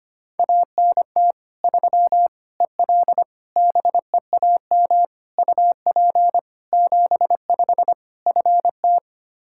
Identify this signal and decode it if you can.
Morse code, heard as a keyed tone, about 25 wpm, 700 Hz. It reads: ANT 3 EL BEAM UP 75 FT
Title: Practice QSO - Exchange 2 - Info - 25wpm